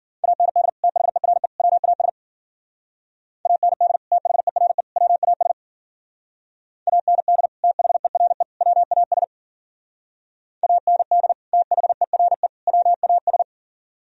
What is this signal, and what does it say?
Morse code, heard as a keyed tone, about 52 wpm, 700 Hz. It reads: AND THERE WAS AND THERE WAS AND THERE WAS AND THERE WAS